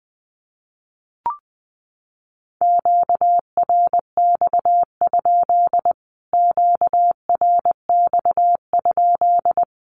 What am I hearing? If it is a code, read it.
Morse code, heard as a keyed tone, about 20 wpm, 700 Hz. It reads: QRX? QRX?
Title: Q-Codes Repeated 3x 20wpm